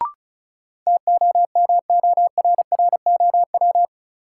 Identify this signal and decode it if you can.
Morse code, heard as a keyed tone, about 34 wpm, 700 Hz. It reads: TOMORROW
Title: Sentences from Top 1100 Words - New Word Drill Alt - 35wpm